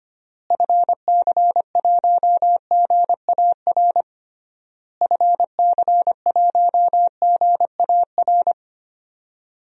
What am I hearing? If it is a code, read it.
Morse code, heard as a keyed tone, about 25 wpm, 700 Hz. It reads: FC1GAR FC1GAR
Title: DXCC-100 Call Signs - 6 Characters - Repeated 3x - 25wpm